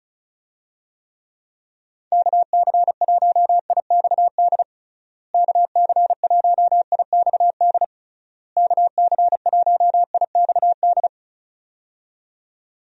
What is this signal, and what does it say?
Morse code, heard as a keyed tone, about 34 wpm, 700 Hz. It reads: KC1IXD KC1IXD KC1IXD
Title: USA Call Signs - 6 Characters - Repeated 3x - 35wpm